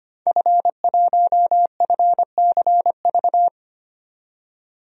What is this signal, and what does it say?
Morse code, heard as a keyed tone, about 25 wpm, 700 Hz. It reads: F1FCV
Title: DXCC-100 Call Signs - 5 Characters - 25wpm